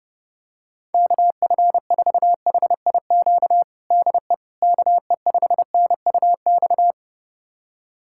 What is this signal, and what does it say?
Morse code, heard as a keyed tone, about 30 wpm, 700 Hz. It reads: KF4HIQ DE KE5NUX
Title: USA Callsign DE Callsign 30wpm